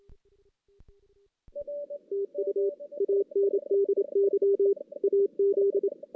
Morse code, heard as a keyed tone, about 27 wpm, 400 Hz. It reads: TU ANDY AZ
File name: 2016 JAN CW NAQP / 2016 Jan naqp k5ot - 0-5